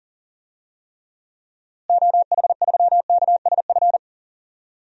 Morse code, heard as a keyed tone, about 40 wpm, 700 Hz. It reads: OH3KSF
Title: DXCC-100 Call Signs - 6 Characters - Repeated 2x - 40wpm